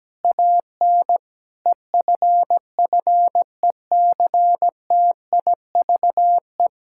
Morse code, heard as a keyed tone, about 17 wpm, 700 Hz. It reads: AN EFFECTIVE
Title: Sentences from Top 1000 Words - Review Part 6 - 17wpm